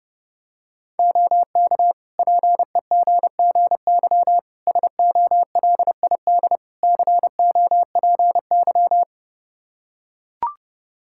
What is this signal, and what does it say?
Morse code, heard as a keyed tone, about 30 wpm, 700 Hz. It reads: OK PEGGY SOLID COPY
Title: Practice QSO - Exchange 3 - More Info - 30wpm